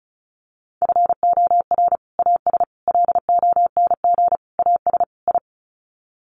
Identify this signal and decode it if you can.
Morse code, heard as a keyed tone, about 34 wpm, 700 Hz. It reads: FOR AS LONG AS I
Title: Sets of 5 Words Encore2 35wpm